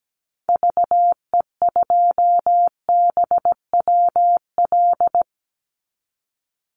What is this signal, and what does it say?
Morse code, heard as a keyed tone, about 17 wpm, 700 Hz. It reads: VE2BWL
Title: Call Signs - 17wpm